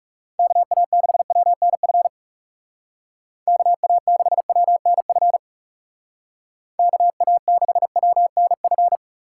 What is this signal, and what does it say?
Morse code, heard as a keyed tone, about 37 wpm, 700 Hz. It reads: KA6WNF KA6WNF KA6WNF
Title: Speed Racing - USA 2x3 Call Signs 30wpm